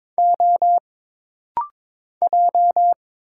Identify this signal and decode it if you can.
Morse code, heard as a keyed tone, about 22 wpm, 700 Hz. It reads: O J
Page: This rapid-fire timing is geared to push your ICR (Instant Character Recognition) skills.